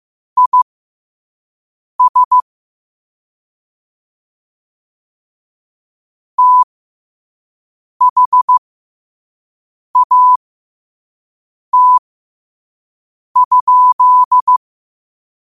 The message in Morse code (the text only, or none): IS THAT?